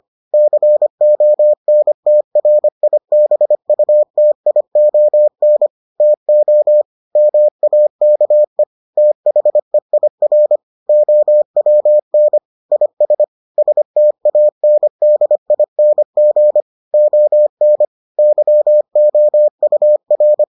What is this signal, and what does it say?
Morse code, heard as a keyed tone, about 25 wpm, 600 Hz. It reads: CONTRIBUTION TO MAKE THEIR OWN IS STANDING ON YOUR